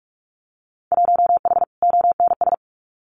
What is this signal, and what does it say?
Morse code, heard as a keyed tone, about 46 wpm, 690 Hz. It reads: 1H ONS
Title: Winter Field Day Exchanges Encore 45wpm